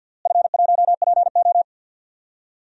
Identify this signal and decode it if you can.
Morse code, heard as a keyed tone, about 47 wpm, 700 Hz. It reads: F1PO
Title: DXCC-100 Call Signs - 4 Characters - Repeated 2x - 50wpm